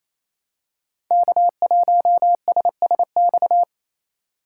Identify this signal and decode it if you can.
Morse code, heard as a keyed tone, about 29 wpm, 700 Hz. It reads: K1SSX
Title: Code Groups - 5 Letter:Numbers - 28wpm